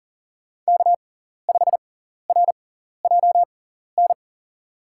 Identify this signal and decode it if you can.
Morse code, heard as a keyed tone, about 38 wpm, spaced as for 14 wpm, 700 Hz. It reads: K5RJN
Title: USA Call Signs - 5 Characters - 6x Character Spacing - 40wpm